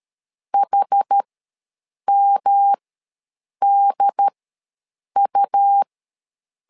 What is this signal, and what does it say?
Morse code, heard as a keyed tone, about 13 wpm, spaced as for 7 wpm, 800 Hz. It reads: HMDU
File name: LEZIONI / 3 MWRO / 3 M ws 12wpm 1745